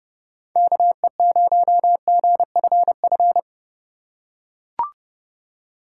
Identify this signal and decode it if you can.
Morse code, heard as a keyed tone, about 30 wpm, 700 Hz. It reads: KE0GFF
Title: USA Call Signs - 6 Characters - 3x Character Spacing - 30wpm